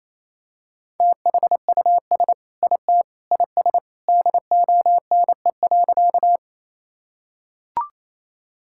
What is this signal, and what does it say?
Morse code, heard as a keyed tone, about 28 wpm, 700 Hz. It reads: THUS IT IS DONE.
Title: Sentences from Top 1000 Words - Review Part 5 - 28wpm